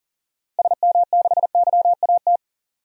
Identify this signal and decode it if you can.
Morse code, heard as a keyed tone, about 40 wpm, 700 Hz. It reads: SM6YAT
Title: DXCC-100 Call Signs - 6 Characters - Repeated 2x - 40wpm